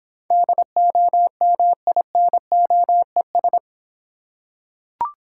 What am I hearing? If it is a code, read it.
Morse code, heard as a keyed tone, about 26 wpm, 700 Hz. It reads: DOMINOES